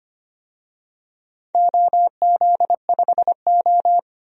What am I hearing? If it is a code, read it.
Morse code, heard as a keyed tone, about 25 wpm, 700 Hz. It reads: OZ5O
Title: DXCC-100 Call Signs - 4 Characters - 25wpm